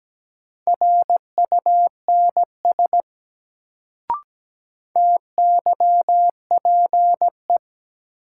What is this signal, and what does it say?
Morse code, heard as a keyed tone, about 17 wpm, 700 Hz. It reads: RUNS TYPE
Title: Sentences from Top 600 Words - New Word Drill - 17wpm